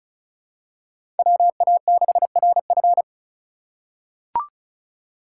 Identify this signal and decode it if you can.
Morse code, heard as a keyed tone, about 34 wpm, 690 Hz. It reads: WA6RF
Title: USA Call Signs - 5 Characters - 2x Character Spacing - 35wpm